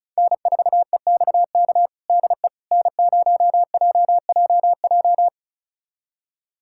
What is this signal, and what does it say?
Morse code, heard as a keyed tone, about 36 wpm, 700 Hz. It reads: N4EXK DE N0JJJ
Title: USA Callsign DE Callsign - 5 Characters - 35wpm